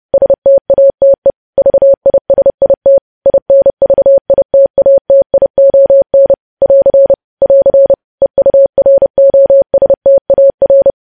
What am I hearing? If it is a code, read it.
Morse code, heard as a keyed tone, about 30 wpm, 550 Hz. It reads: STATE VISIT INVITATION <AR> <AR> EUROSTAR
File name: podcast / News headlines-1x-30x30